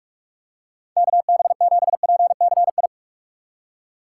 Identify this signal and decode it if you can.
Morse code, heard as a keyed tone, about 45 wpm, 700 Hz. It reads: KB7PCI